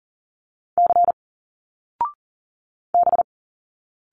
Morse code, heard as a keyed tone, about 38 wpm, 700 Hz. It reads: C B
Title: Single Letters 40wpm